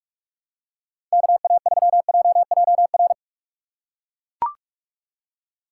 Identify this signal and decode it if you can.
Morse code, heard as a keyed tone, about 42 wpm, 700 Hz. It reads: KA3JJR
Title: USA Call Signs - 6 Characters - 6x Character Spacing - 45wpm